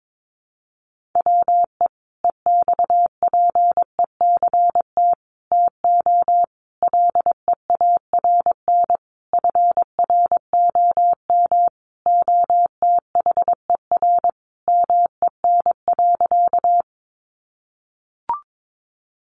Morse code, heard as a keyed tone, about 22 wpm, 700 Hz. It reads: WE EXPECT TO LEARN FROM OTHER MEN.
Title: Sentences from Top 1100 Words 22wpm